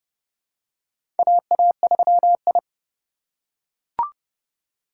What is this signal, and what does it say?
Morse code, heard as a keyed tone, about 30 wpm, 700 Hz. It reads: AA3I
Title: USA Call Signs - 4 Characters - 6x Character Spacing - 30wpm